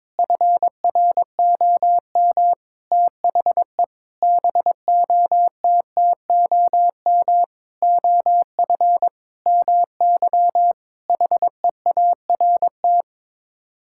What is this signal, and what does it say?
Morse code, heard as a keyed tone, about 22 wpm, 700 Hz. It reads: FROM THE BOTTOM OF MY HEART
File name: practice / 22wpm / Sets of 8 Words - 22wpm